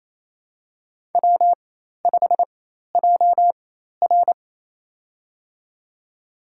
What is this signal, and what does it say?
Morse code, heard as a keyed tone, about 28 wpm, spaced as for 13 wpm, 700 Hz. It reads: W5JR
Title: USA Call Signs - 4 Characters - 4x Character Spacing - 28wpm